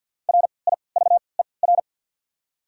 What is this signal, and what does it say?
Morse code, heard as a keyed tone, about 54 wpm, spaced as for 29 wpm, 700 Hz. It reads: RIVER
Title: 3x Character Spacing - Top 500 Words - 50wpm